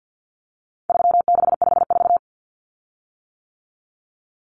Morse code, heard as a keyed tone, about 50 wpm, 700 Hz. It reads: 3654
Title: Common Cut Numbers - 4-Digit 50wpm